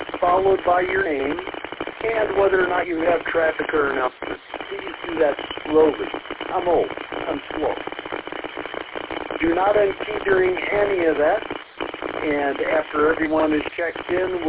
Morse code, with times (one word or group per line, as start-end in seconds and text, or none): none